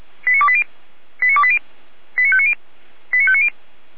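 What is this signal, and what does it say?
Five-tone selective call (ZVEI), heard as quick runs of five tones, tones of about 70 ms. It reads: 89280 89280 89590 89590
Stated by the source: Five-tone ZVEI selective call received by SDR.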